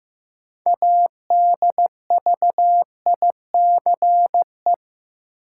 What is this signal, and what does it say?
Morse code, heard as a keyed tone, about 16 wpm, 700 Hz. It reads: ADVICE
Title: Sentences from Top 1100 Words - New Word Drill Alt - 15wpm